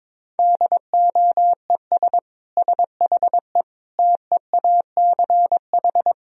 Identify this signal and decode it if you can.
Morse code, heard as a keyed tone, about 22 wpm, 700 Hz. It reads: DOES SHE TEACH
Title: Sentences from Top 1000 Words - Review Part 3 - 22wpm